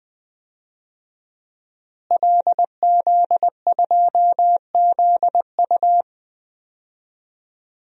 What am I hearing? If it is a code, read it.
Morse code, heard as a keyed tone, about 20 wpm, 700 Hz. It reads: LZ2ZU
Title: DXCC-100 Call Signs - 5 Characters - Repeated 2x - 20wpm